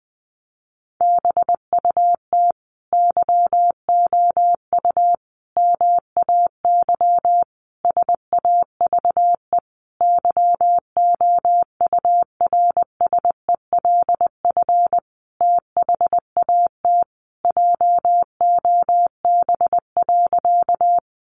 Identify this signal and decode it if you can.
Morse code, heard as a keyed tone, about 20 wpm, 700 Hz. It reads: BUT YOU MAY SAVE YOURSELF THAT JOB.
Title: Sentences from Top 1100 Words 20wpm